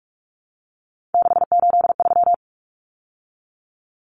Morse code, heard as a keyed tone, about 44 wpm, 700 Hz. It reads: 683
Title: Numbers - 3-Digit 45wpm